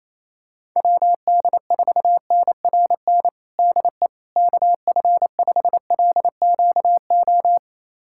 Morse code, heard as a keyed tone, about 27 wpm, 700 Hz. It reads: WD4NRN DE KF5LQO
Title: USA Callsign DE Callsign - 6 Characters - 28wpm